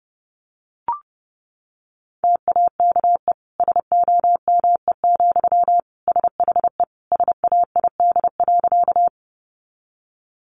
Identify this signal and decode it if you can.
Morse code, heard as a keyed tone, about 30 wpm, 700 Hz. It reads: TAKE SOME, SHE SAID.
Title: Sentences from Top 200 Words 30wpm